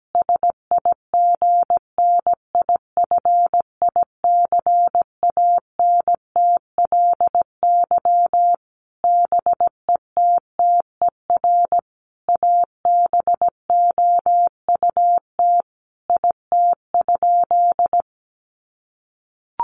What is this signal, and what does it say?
Morse code, heard as a keyed tone, about 17 wpm, 700 Hz. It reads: SIGNIFICANTLY BETTER ABOUT IT?
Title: Sentences from Top 1500 Words 17wpm